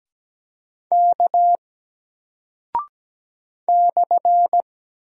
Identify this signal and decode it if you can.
Morse code, heard as a keyed tone, about 17 wpm, 700 Hz. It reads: K /
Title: Single Character (Letter:Number:Punctuation:Prosign) Alt 17wpm